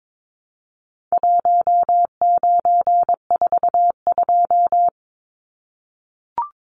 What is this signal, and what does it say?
Morse code, heard as a keyed tone, about 22 wpm, 700 Hz. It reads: 1942
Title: Numbers - Contest Style - 4-Digit 22wpm